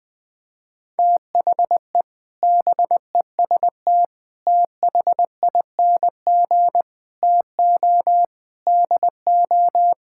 Morse code, heard as a keyed tone, about 20 wpm, 700 Hz. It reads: THE BEST THING TO DO
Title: Sentences with 8-Word Phrases - 20wpm